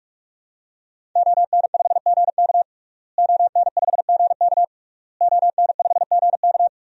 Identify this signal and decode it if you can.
Morse code, heard as a keyed tone, about 46 wpm, 700 Hz. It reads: ON5GK ON5GK ON5GK
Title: DXCC-100 Call Signs - 5 Characters - Repeated 3x - 45wpm